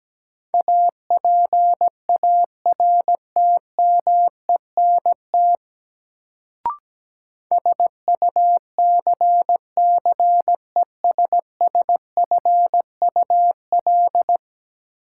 Morse code, heard as a keyed tone, about 17 wpm, 700 Hz. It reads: APARTMENT SUCCESSFUL
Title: Sentences from Top 900 Words - New Word Drill - 17wpm